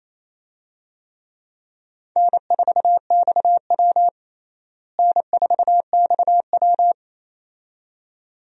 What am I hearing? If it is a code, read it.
Morse code, heard as a keyed tone, about 28 wpm, 700 Hz. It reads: N4XW N4XW
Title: USA Call Signs - 4 Characters - Repeated 2x - 28wpm